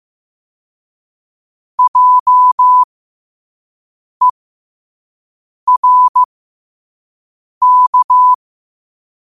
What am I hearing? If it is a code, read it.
Morse code, heard as a keyed tone, about 15 wpm, spaced as for 5 wpm, 1000 Hz. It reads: JERK
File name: qotdfiles / 5wpm